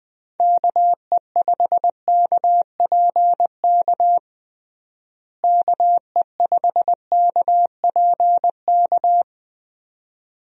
Morse code, heard as a keyed tone, about 21 wpm, 700 Hz. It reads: KE5KPK KE5KPK
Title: USA Call Signs - 6 Characters - Repeated 2x - 20wpm